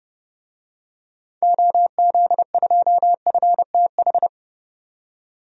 Morse code, heard as a keyed tone, about 30 wpm, 700 Hz. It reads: OZ2FTH
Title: DXCC-100 Call Signs - 6 Characters - 30wpm